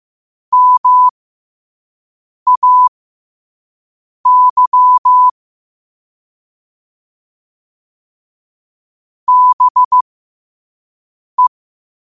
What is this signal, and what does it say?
Morse code, heard as a keyed tone, about 15 wpm, spaced as for 5 wpm, 1000 Hz. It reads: MAY BE
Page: New quotes every day in morse code at 5 Words per minute.